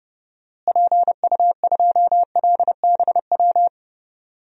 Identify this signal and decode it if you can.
Morse code, heard as a keyed tone, about 32 wpm, 700 Hz. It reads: PU2LBW
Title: DXCC-100 Call Signs - 6 Characters - Repeated 3x - 30wpm